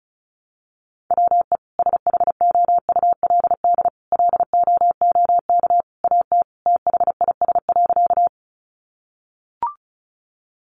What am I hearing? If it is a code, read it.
Morse code, heard as a keyed tone, about 34 wpm, 700 Hz. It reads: WE SHOULD LOOK AT THIS.
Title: Sentences from Top 300 Words 35wpm